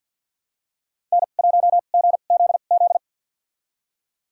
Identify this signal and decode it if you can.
Morse code, heard as a keyed tone, about 52 wpm, spaced as for 37 wpm, 700 Hz. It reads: N1GZZ
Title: USA Call Signs - 5 Characters - 2x Character Spacing - 50wpm